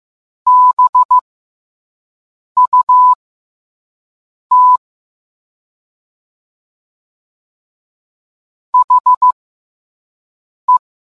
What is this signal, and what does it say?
Morse code, heard as a keyed tone, about 15 wpm, spaced as for 5 wpm, 1000 Hz. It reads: BUT HE